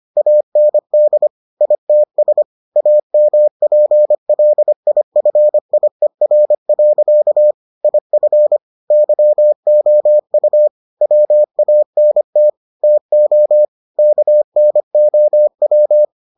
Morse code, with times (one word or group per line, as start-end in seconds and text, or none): 0.2-1.3 AND
1.6-2.4 ITS
2.8-7.5 AMPLIFIER.
7.8-8.6 IF
8.9-10.7 YOU
11.0-12.5 WANT
12.8-13.7 TO
14.0-16.0 KNOW